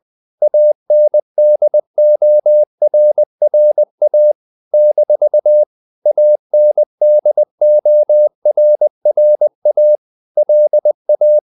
Morse code, heard as a keyed tone, about 20 wpm, 600 Hz. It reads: ANDORRA - ANDORRA LA
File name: CW / Practice / NationalCapitals20